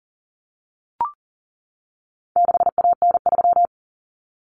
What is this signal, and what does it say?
Morse code, heard as a keyed tone, about 40 wpm, 700 Hz. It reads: 6AN3
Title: Common Cut Numbers - 4-Digit 40wpm